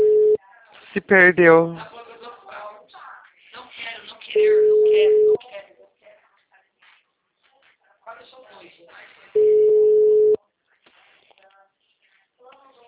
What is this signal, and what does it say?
Ringback tone (Europe, 425 Hz), with speech.
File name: out- -2033- - - .6849